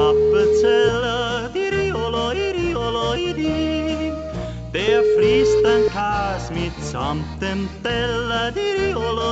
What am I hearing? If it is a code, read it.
Ringback tone (Europe, 425 Hz), with noise.